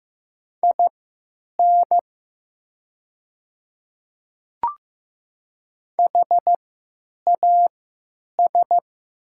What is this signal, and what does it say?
Morse code, heard as a keyed tone, about 15 wpm, spaced as for 9 wpm, 700 Hz. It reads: IN HAS
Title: 3x Character Spacing - Top 100 Words - 15wpm